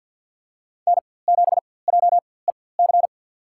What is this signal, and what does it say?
Morse code, heard as a keyed tone, about 50 wpm, spaced as for 23 wpm, 700 Hz. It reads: N7JEC
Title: USA Call Signs - 5 Characters - 4x Character Spacing - 50wpm